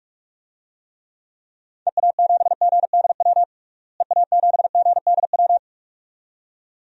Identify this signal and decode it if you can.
Morse code, heard as a keyed tone, about 46 wpm, 700 Hz. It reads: EA7GDW EA7GDW
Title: DXCC-100 Call Signs - 6 Characters - Repeated 2x - 45wpm